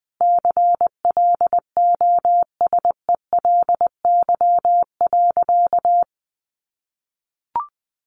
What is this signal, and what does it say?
Morse code, heard as a keyed tone, about 20 wpm, 700 Hz. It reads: CLOSELY.
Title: Sentences from Top 1400 Words 20wpm